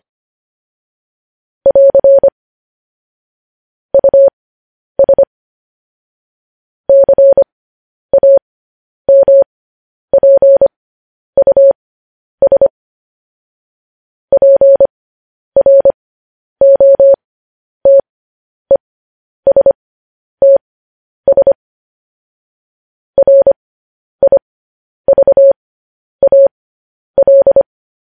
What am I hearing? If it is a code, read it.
Morse code, heard as a keyed tone, about 25 wpm, spaced as for 10 wpm, 550 Hz. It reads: <AR> US CAMPUS PROTESTS RIVAL
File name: podcast / News headlines-1x-25x10